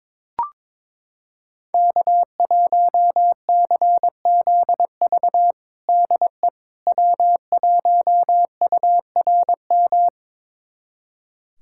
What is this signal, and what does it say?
Morse code, heard as a keyed tone, about 22 wpm, 700 Hz. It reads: K1CZV DE W1URM
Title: USA Callsign DE Callsign - 5 Characters - 22wpm